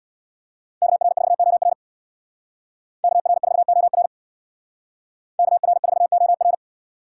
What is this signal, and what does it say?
Morse code, heard as a keyed tone, about 57 wpm, 700 Hz. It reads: KR4OR KR4OR KR4OR
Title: Speed Racing - USA Call Signs 50wpm